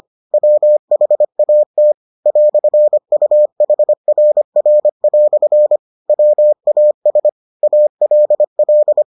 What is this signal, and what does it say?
Morse code, heard as a keyed tone, about 25 wpm, 600 Hz. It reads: WHAT "UHRR" WAS ALL